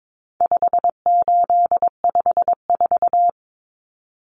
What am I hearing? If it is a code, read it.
Morse code, heard as a keyed tone, about 22 wpm, 700 Hz. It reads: 5854
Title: Common Cut Numbers - 4-Digit 22wpm